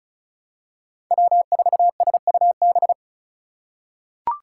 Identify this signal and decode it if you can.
Morse code, heard as a keyed tone, about 34 wpm, 700 Hz. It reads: W4SUB
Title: USA Call Signs - 5 Characters - 3x Character Spacing - 35wpm